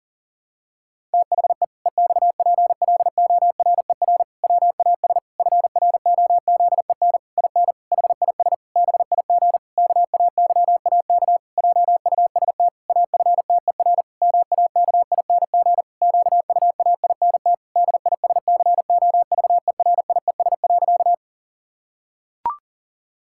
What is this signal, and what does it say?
Morse code, heard as a keyed tone, about 37 wpm, 700 Hz. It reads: THE EXPLORER WAS FROZEN IN HIS BIG KAYAK JUST AFTER MAKING QUAINT DISCOVERIES.
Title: Pangram Challenge v2 - 40wpm